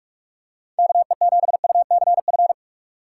Morse code, heard as a keyed tone, about 43 wpm, 700 Hz. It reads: KE7UCF
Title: USA Call Signs - 6 Characters - 6x Character Spacing - 45wpm